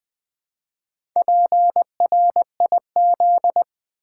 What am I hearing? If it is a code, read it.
Morse code, heard as a keyed tone, about 20 wpm, 700 Hz. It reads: PRIZ
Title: Build A Word - 1 Letter At A Time - 3 to 5-Letter Words - 20wpm